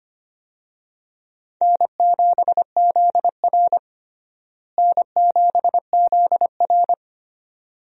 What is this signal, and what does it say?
Morse code, heard as a keyed tone, about 25 wpm, 700 Hz. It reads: N7ZR N7ZR
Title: USA Call Signs - 4 Characters - Repeated 2x - 25wpm